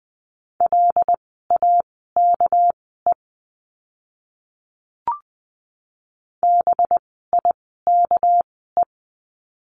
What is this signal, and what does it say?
Morse code, heard as a keyed tone, about 19 wpm, spaced as for 14 wpm, 700 Hz. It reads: LAKE BIKE
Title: 199 Lesson 32f - Letter k - Words 20wpm